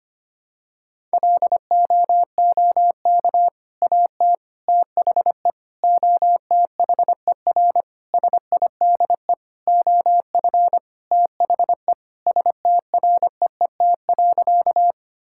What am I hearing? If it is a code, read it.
Morse code, heard as a keyed tone, about 25 wpm, 700 Hz. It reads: LOOK AT THE OTHER SIDE OF THE STREET.
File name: practice / 25wpm / Sentences with 3-Word Phrases - 25wpm